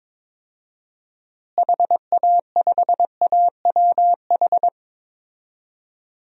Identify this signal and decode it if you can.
Morse code, heard as a keyed tone, about 22 wpm, 700 Hz. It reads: HA5AWH
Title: DXCC-100 Call Signs - 6 Characters - Repeated 3x - 22wpm